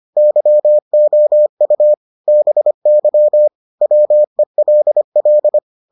Morse code, heard as a keyed tone, about 25 wpm, 600 Hz. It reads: YOU BY WELL